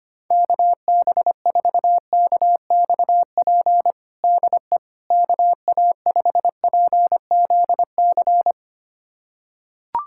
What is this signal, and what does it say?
Morse code, heard as a keyed tone, about 25 wpm, 700 Hz. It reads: KB4KXP DE KA5PZC
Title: USA Callsign DE Callsign - 6 Characters - 25wpm